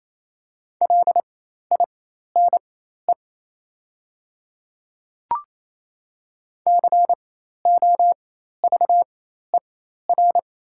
Morse code, heard as a keyed tone, about 28 wpm, spaced as for 13 wpm, 700 Hz. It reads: LINE COVER
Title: 4x Character Spacing - Top 300 Words - 28wpm